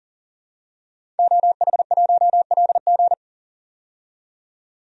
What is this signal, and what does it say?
Morse code, heard as a keyed tone, about 40 wpm, 700 Hz. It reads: OH1LG
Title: DXCC-100 Call Signs - 5 Characters - 40wpm